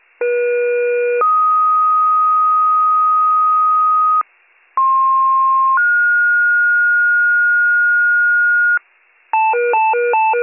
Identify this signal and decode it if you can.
Two-tone page, heard as a one-second tone, then a three-second tone, about 480 then 1250 Hz, about 1060 then 1430 Hz, with noise. It is heard from 0.2 to 8.8 s.